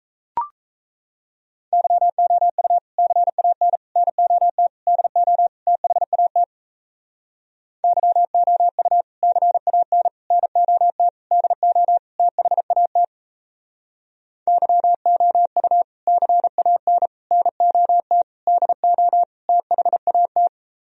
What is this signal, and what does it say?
Morse code, heard as a keyed tone, about 36 wpm, 700 Hz. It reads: YOU CAN NOT DO THAT YOU CAN NOT DO THAT YOU CAN NOT DO THAT